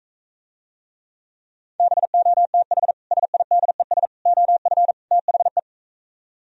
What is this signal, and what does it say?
Morse code, heard as a keyed tone, about 45 wpm, 700 Hz. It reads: BOTH SIDES OF THE